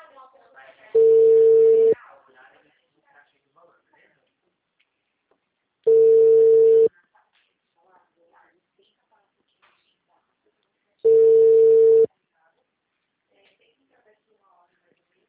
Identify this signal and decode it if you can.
Ringback tone (China, 450 Hz), with speech.